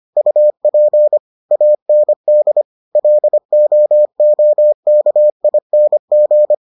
Morse code, heard as a keyed tone, about 25 wpm, 600 Hz. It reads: UP AND LOOKING